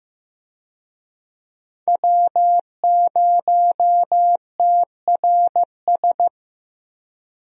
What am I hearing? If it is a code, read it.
Morse code, heard as a keyed tone, about 15 wpm, 700 Hz. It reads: W0TRS
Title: USA Call Signs - 5 Characters - Repeated 2x - 15wpm